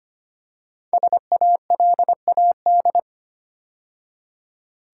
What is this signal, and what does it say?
Morse code, heard as a keyed tone, about 25 wpm, 700 Hz. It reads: SALAD
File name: 5 Letter Words - First 3 Letters Given - 25wpm